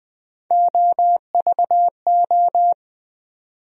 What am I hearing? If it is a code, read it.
Morse code, heard as a keyed tone, about 20 wpm, 700 Hz. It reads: OVO
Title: USA Call Sign Postfixes - 3 Letters 20wpm